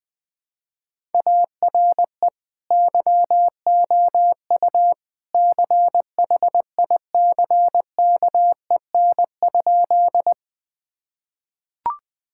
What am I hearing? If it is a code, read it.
Morse code, heard as a keyed tone, about 21 wpm, 700 Hz. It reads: ARE YOU CHICKEN?
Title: Sentences from Top 1000 Words - Review Part 5 - 20wpm